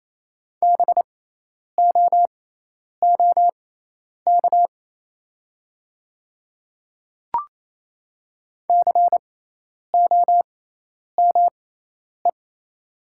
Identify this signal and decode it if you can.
Morse code, heard as a keyed tone, about 28 wpm, spaced as for 10 wpm, 700 Hz. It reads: BOOK COME
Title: 6x Character Spacing - Top 500 Words - 28wpm